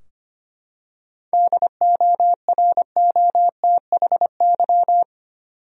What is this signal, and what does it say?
Morse code, heard as a keyed tone, about 25 wpm, 700 Hz. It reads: DOROTHY
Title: USA Common Names - 25wpm